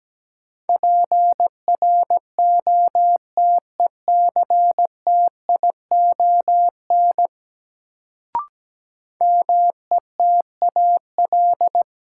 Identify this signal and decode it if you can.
Morse code, heard as a keyed tone, about 17 wpm, 700 Hz. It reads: PROTECTION METAL
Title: Sentences from Top 1600 Words - New Word Drill Alt - 17wpm